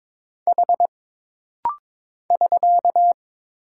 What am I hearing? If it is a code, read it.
Morse code, heard as a keyed tone, about 22 wpm, 700 Hz. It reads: H <SK>
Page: This rapid-fire timing is geared to push your ICR (Instant Character Recognition) skills.